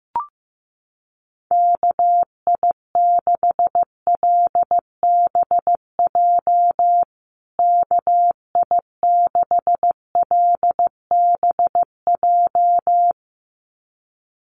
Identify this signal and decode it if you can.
Morse code, heard as a keyed tone, about 15 wpm, 700 Hz. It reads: KI6LBJ KI6LBJ
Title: USA Call Signs Repeated 2x 15wpm